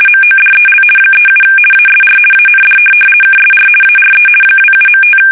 RTTY, about 50 baud. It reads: MODEL 37 ASR AND KSR ARE 27.5
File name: fsk850-50B / teletype-corp